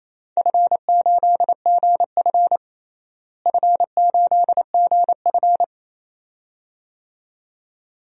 Morse code, heard as a keyed tone, about 28 wpm, 700 Hz. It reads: F8GF F8GF
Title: DXCC-100 Call Signs - 4 Characters - Repeated 2x - 28wpm